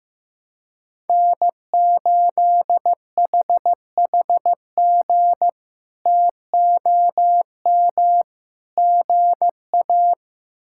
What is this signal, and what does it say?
Morse code, heard as a keyed tone, about 15 wpm, 700 Hz. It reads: N8HHG TOM GA
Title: K1USN SST QSO Practice - 15wpm